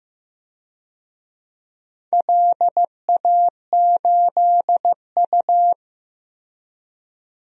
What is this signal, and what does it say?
Morse code, heard as a keyed tone, about 15 wpm, 700 Hz. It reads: LA8U
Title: DXCC-100 Call Signs - 4 Characters - 15wpm